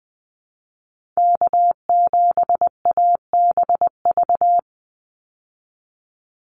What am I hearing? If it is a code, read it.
Morse code, heard as a keyed tone, about 20 wpm, 700 Hz. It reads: K7ABV
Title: Call Signs Repeated 3x - 20wpm